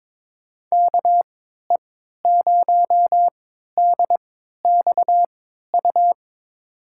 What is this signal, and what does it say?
Morse code, heard as a keyed tone, about 22 wpm, spaced as for 12 wpm, 700 Hz. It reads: KE0DXU
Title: USA Call Signs - 6 Characters - 3x Character Spacing - 22wpm